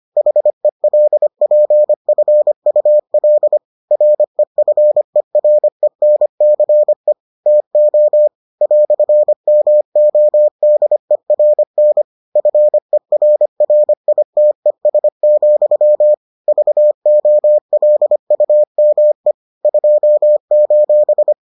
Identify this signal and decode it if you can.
Morse code, heard as a keyed tone, about 25 wpm, 600 Hz. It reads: HELPFUL REFERENCE TO "MODERN FERRITES, VOLUME 2: